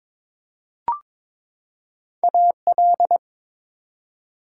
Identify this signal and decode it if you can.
Morse code, heard as a keyed tone, about 22 wpm, 700 Hz. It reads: AL
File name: ARRL Sections v2 - 22wpm